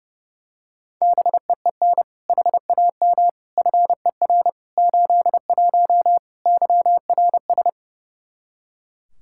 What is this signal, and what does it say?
Morse code, heard as a keyed tone, about 30 wpm, 700 Hz. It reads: BEEN HAM FER 81 YRS
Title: Practice QSO - Exchange 3 - More Info - 30wpm